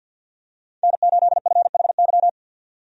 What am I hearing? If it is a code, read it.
Morse code, heard as a keyed tone, about 46 wpm, 700 Hz. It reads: N8FHY
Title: USA Call Signs - 5 Characters - 50wpm